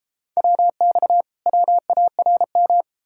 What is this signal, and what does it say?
Morse code, heard as a keyed tone, about 39 wpm, 700 Hz. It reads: WX WARM